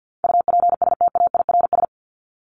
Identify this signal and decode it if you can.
Morse code, heard as a keyed tone, about 50 wpm, 700 Hz. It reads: UPSTAIRS
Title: Top 1751-2000 Words 50wpm